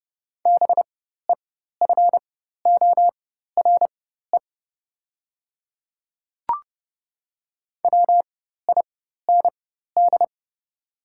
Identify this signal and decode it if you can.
Morse code, heard as a keyed tone, about 30 wpm, spaced as for 14 wpm, 700 Hz. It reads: BEFORE WIND
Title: 4x Character Spacing - Top 500 Words - 30wpm